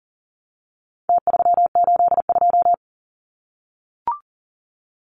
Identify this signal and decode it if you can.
Morse code, heard as a keyed tone, about 40 wpm, 700 Hz. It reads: T382
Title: Numbers - Contest Style - 4-Digit 40wpm